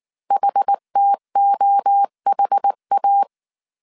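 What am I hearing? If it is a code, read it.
Morse code, heard as a keyed tone, about 19 wpm, 800 Hz. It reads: HTOHA
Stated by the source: Morse code dictation at 18 wpm from the IW3SGT QRS Morse code training course, five-character groups of 300 characters.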